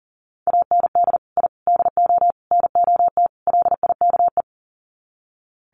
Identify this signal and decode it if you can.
Morse code, heard as a keyed tone, about 40 wpm, 700 Hz. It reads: AND I DO NOT LIKE
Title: Sets of 5 Words Encore2 40wpm